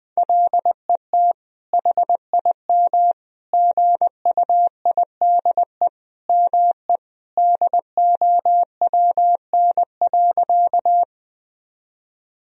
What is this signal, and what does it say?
Morse code, heard as a keyed tone, about 20 wpm, 700 Hz. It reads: LET HIM GUIDE ME DOWN.
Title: Sentences from Top 1000 Words - Review Part 3 - 20wpm